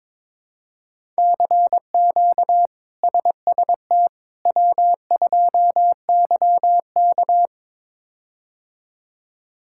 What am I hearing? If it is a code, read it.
Morse code, heard as a keyed tone, about 23 wpm, 700 Hz. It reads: CQ SST W2YK
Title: K1USN SST QSO Practice - 22wpm